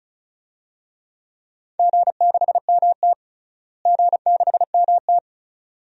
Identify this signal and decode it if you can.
Morse code, heard as a keyed tone, about 36 wpm, 700 Hz. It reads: G6MT G6MT
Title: DXCC-100 Call Signs - 4 Characters - Repeated 2x - 35wpm